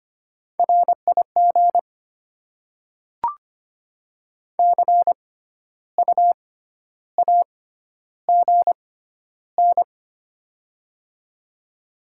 Morse code, heard as a keyed tone, about 25 wpm, spaced as for 12 wpm, 700 Hz. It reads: RIG CU AGN
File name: Top 20 QSO Elements - 6x Character Spacing - 25wpm